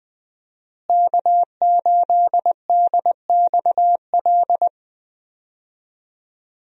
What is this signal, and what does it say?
Morse code, heard as a keyed tone, about 20 wpm, 700 Hz. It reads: K8DXL
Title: USA Call Signs - 5 Characters - Repeated 3x - 20wpm